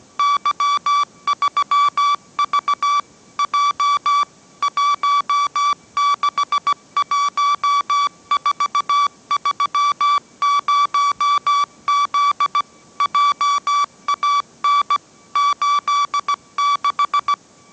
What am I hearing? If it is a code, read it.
Morse code, heard as a keyed tone, about 17 wpm, 1150 Hz. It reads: Y3V J 161430Z JAN 86
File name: mrx / mrx080 18wpm